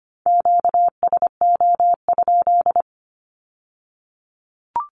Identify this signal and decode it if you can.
Morse code, heard as a keyed tone, about 25 wpm, 700 Hz. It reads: QSO?
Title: Q-Codes with Spoken Meaning - 25wpm